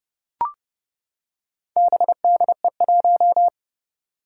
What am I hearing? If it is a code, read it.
Morse code, heard as a keyed tone, about 31 wpm, 700 Hz. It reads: BDE1
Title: Four Challenging Letter-Number Combinations 30wpm